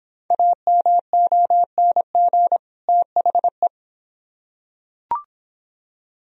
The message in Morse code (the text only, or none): AMONG THE